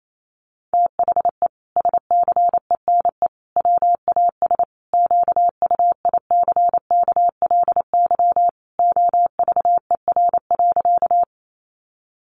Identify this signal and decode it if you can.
Morse code, heard as a keyed tone, about 28 wpm, 700 Hz. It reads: THE SCENE WAS QUICKLY OVER.
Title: Sentences from Top 1100 Words 28wpm